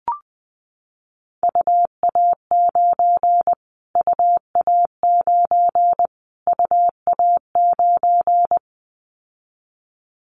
Morse code, heard as a keyed tone, about 20 wpm, 700 Hz. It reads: UA9 UA9 UA9
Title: Callsign Prefixes for Top 25 DXCC - Repeated 3x - 20wpm